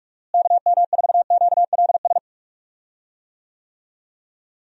Morse code, heard as a keyed tone, about 44 wpm, 700 Hz. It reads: KM4QLS
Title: USA Call Signs - 6 Characters - Repeated 2x - 45wpm